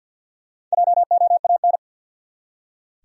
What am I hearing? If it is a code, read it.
Morse code, heard as a keyed tone, about 45 wpm, 700 Hz. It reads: JOAN
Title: USA Common Names - Max 4 Letters - 50wpm